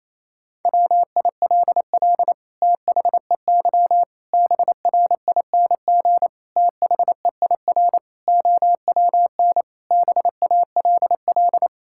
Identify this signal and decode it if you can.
Morse code, heard as a keyed tone, about 28 wpm, 700 Hz. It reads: WILL THEY BRING THEIR OWN BALL
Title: Sentences from Top 1000 Words - Review Part 6 - 28wpm